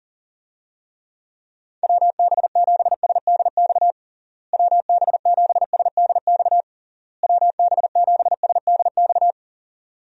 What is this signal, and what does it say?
Morse code, heard as a keyed tone, about 40 wpm, 700 Hz. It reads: WB7SDX WB7SDX WB7SDX
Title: USA Call Signs - 6 Characters - Repeated 3x - 40wpm